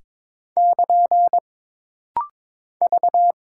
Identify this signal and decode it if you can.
Morse code, heard as a keyed tone, about 22 wpm, 700 Hz. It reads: <KN> V
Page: This rapid-fire timing is geared to push your ICR (Instant Character Recognition) skills.